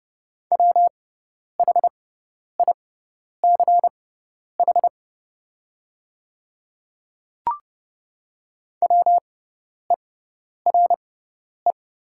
Morse code, heard as a keyed tone, about 30 wpm, spaced as for 10 wpm, 700 Hz. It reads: WHICH WERE
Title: 6x Character Spacing - Top 200 Words - 30wpm